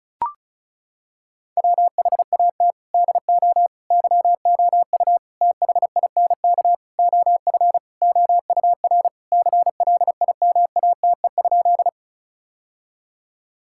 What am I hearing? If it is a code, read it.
Morse code, heard as a keyed tone, about 34 wpm, 700 Hz. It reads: WHAT DO YOU THINK OF OUR CLIMATE?
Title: Sentences from Top 1600 Words 35wpm